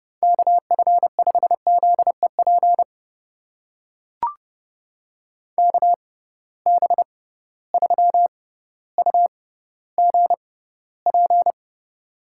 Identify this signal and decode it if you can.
Morse code, heard as a keyed tone, about 30 wpm, spaced as for 15 wpm, 700 Hz. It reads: KF5ZEP KB3UGP
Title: USA Call Signs - 6 Characters - 6x Character Spacing - 30wpm